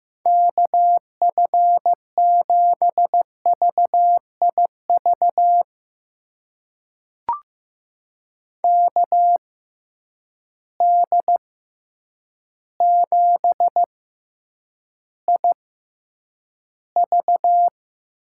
Morse code, heard as a keyed tone, about 15 wpm, spaced as for 8 wpm, 700 Hz. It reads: KF7VIV KD7IV
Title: USA Call Signs - 6 Characters - 6x Character Spacing - 15wpm